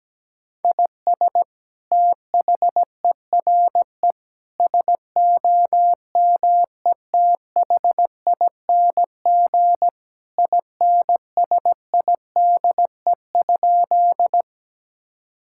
Morse code, heard as a keyed tone, about 17 wpm, 700 Hz. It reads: IS THERE SOMETHING INSIDE?
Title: Sentences from Top 1000 Words - Review Part 6 - 17wpm